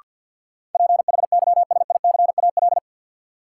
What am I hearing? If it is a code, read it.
Morse code, heard as a keyed tone, about 51 wpm, 700 Hz. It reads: PHYSICAL
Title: Sentences from Top 1400 Words - New Word Drill Alt - 50wpm